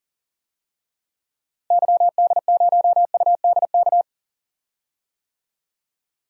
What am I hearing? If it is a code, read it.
Morse code, heard as a keyed tone, about 40 wpm, 700 Hz. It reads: YD0UDK
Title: DXCC-100 Call Signs - 6 Characters - Repeated 3x - 40wpm